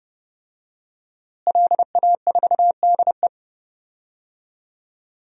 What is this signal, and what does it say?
Morse code, heard as a keyed tone, about 30 wpm, 700 Hz. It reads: LA4DE
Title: DXCC-100 Call Signs - 5 Characters - Repeated 2x - 30wpm